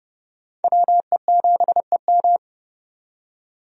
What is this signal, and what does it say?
Morse code, heard as a keyed tone, about 30 wpm, 700 Hz. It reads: WE7EM
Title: USA Call Signs - 5 Characters - 30wpm